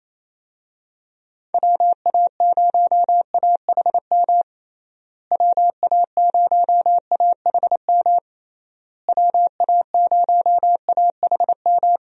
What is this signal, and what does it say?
Morse code, heard as a keyed tone, about 27 wpm, 700 Hz. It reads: WA0AHM WA0AHM WA0AHM
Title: USA Call Signs - 6 Characters - Repeated 3x - 28wpm